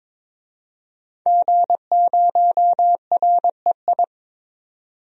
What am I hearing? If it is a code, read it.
Morse code, heard as a keyed tone, about 22 wpm, 700 Hz. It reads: G0REI
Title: DXCC-100 Call Signs - 5 Characters - 22wpm